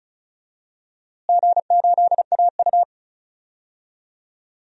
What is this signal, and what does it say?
Morse code, heard as a keyed tone, about 36 wpm, 700 Hz. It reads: G8AU
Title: DXCC-100 Call Signs - 4 Characters - Repeated 2x - 35wpm